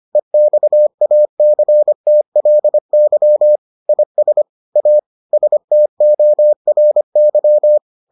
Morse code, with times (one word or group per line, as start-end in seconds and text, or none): 0.2-3.6 EXACTLY
3.9-4.4 IS
4.8-5.0 A
5.3-7.8 STORY